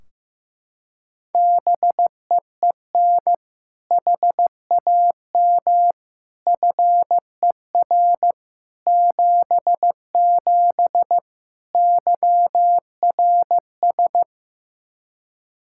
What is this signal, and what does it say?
Morse code, heard as a keyed tone, about 16 wpm, 700 Hz. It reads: BEEN HAM FER 77 YRS
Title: Practice QSO - Exchange 3 - More Info - 15wpm